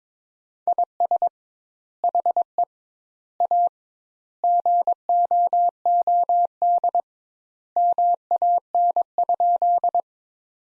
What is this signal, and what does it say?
Morse code, heard as a keyed tone, about 22 wpm, 700 Hz. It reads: IS HE A GOOD MAN?
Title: Sentences Top 200 Words - 22wpm